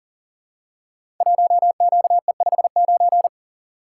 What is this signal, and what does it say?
Morse code, heard as a keyed tone, about 43 wpm, 700 Hz. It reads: 1QE59
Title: Code Groups - 5 Letter:Numbers - 40wpm